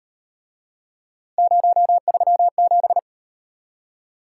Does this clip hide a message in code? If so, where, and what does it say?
Yes: Morse code from 1.4 to 3.0 s, heard as a keyed tone, about 38 wpm, 700 Hz. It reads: 037